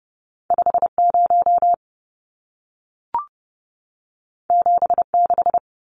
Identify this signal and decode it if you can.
Morse code, heard as a keyed tone, about 30 wpm, 700 Hz. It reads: 50 76
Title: Numbers - 2-Digit 30wpm